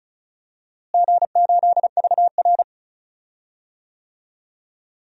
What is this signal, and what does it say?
Morse code, heard as a keyed tone, about 36 wpm, 700 Hz. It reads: G8VR
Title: DXCC-100 Call Signs - 4 Characters - Repeated 2x - 35wpm